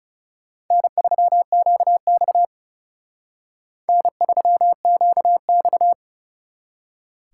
Morse code, heard as a keyed tone, about 38 wpm, 700 Hz. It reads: N3QX N3QX
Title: Speed Racing - USA Call Signs 30wpm